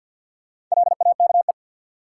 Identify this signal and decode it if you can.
Morse code, heard as a keyed tone, about 53 wpm, 700 Hz. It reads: LAKE
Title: Sentences from Top 1000 Words - Review Part 3 - New Word Drill - 50wpm